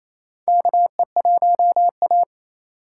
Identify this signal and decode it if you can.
Morse code, heard as a keyed tone, about 28 wpm, 700 Hz. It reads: KE1A
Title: USA Call Signs - 4 Characters - 8x Character Spacing - 28wpm